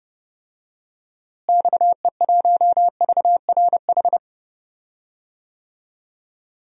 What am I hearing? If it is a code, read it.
Morse code, heard as a keyed tone, about 31 wpm, 700 Hz. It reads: XE1VRH
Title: DXCC-100 Call Signs - 6 Characters - Repeated 3x - 30wpm